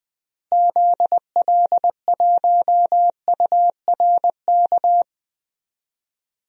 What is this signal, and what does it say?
Morse code, heard as a keyed tone, about 20 wpm, 700 Hz. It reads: ZL1URK
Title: DXCC-100 Call Signs - 6 Characters - Repeated 3x - 20wpm